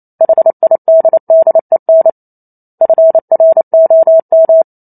Morse code, heard as a keyed tone, about 28 wpm, 650 Hz. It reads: HIDDEN FROM